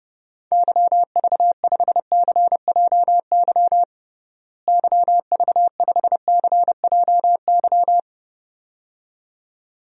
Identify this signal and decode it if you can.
Morse code, heard as a keyed tone, about 32 wpm, 700 Hz. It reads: YV5CJY YV5CJY
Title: DXCC-100 Call Signs - 6 Characters - Repeated 2x - 30wpm